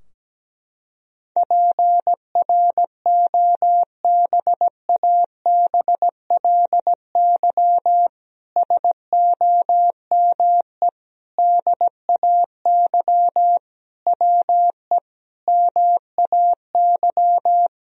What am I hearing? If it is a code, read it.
Morse code, heard as a keyed tone, about 17 wpm, 700 Hz. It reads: PROBABLY SOME DAY WE MAY
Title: Sentences from Top 1000 Words - Review Part 4 - 17wpm